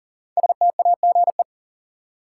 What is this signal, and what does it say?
Morse code, heard as a keyed tone, about 40 wpm, 700 Hz. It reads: STAGE
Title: Sentences from Top 1200 Words - New Word Drill Alt - 40wpm